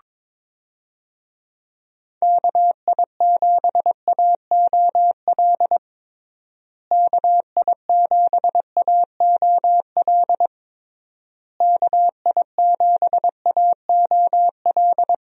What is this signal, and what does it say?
Morse code, heard as a keyed tone, about 21 wpm, 700 Hz. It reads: KI7AOL KI7AOL KI7AOL
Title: USA Call Signs - 6 Characters - Repeated 3x - 22wpm